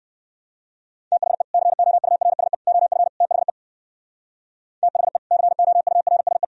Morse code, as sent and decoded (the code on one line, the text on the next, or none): - .... / -.-. --- ..- .-. ... / --- ..-. / - .... / - .... / -.-. --- ..- .-. ... .
THE COURSE OF THE THE COURSE